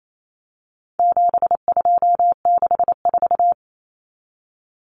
Morse code, heard as a keyed tone, about 28 wpm, 700 Hz. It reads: 7264
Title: Numbers - 4-Digit 28wpm